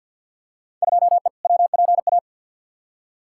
Morse code, heard as a keyed tone, about 51 wpm, 700 Hz. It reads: 2E WPA
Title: Field Day Exchanges - 50wpm